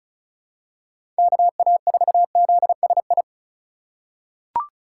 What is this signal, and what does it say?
Morse code, heard as a keyed tone, about 36 wpm, 700 Hz. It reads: KA4ZSI
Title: USA Call Signs - 6 Characters - 2x Character Spacing - 35wpm